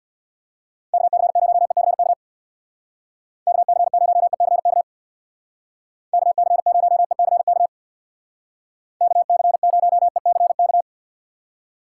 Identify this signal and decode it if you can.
Morse code, heard as a keyed tone, about 64 wpm, 700 Hz. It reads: KC0ECK KC0ECK KC0ECK KC0ECK